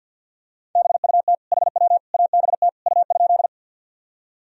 Morse code, heard as a keyed tone, about 47 wpm, 700 Hz. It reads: BUT HW ABT U?
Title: Sentences from Top 100 Words and Abbreviations 50wpm